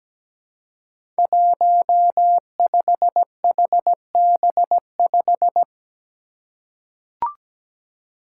Morse code, heard as a keyed tone, about 17 wpm, 700 Hz. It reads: 15HB5
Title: Five Challenging Letter-Number Combinations 17wpm